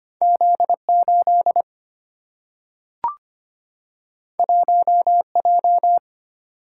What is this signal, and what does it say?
Morse code, heard as a keyed tone, about 25 wpm, 700 Hz. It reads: Z 8 1 J
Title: Single-Letter Single-Number 25wpm